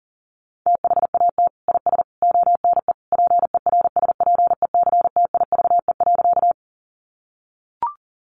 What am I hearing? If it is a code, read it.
Morse code, heard as a keyed tone, about 37 wpm, 700 Hz. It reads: THAT IS ONE PERSPECTIVE.
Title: Sentences from Top 1200 Words 40wpm